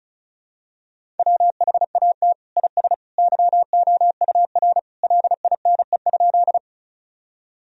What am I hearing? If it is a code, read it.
Morse code, heard as a keyed tone, about 36 wpm, 700 Hz. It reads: WHAT IS YOUR LINE?
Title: Sentences from Top 1000 Words - Review Part 1 - 35wpm